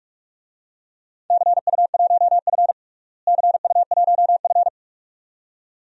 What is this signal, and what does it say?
Morse code, heard as a keyed tone, about 46 wpm, 700 Hz. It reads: CU1F CU1F
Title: DXCC-100 Call Signs - 4 Characters - Repeated 2x - 45wpm